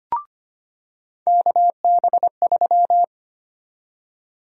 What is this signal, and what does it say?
Morse code, heard as a keyed tone, about 25 wpm, 700 Hz. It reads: KB3
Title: USA Call Sign Prefixes - 25wpm